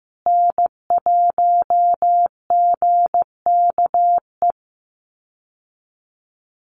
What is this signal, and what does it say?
Morse code, heard as a keyed tone, about 15 wpm, 700 Hz. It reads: N1GKE
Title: Call Signs Repeated 3x - 15wpm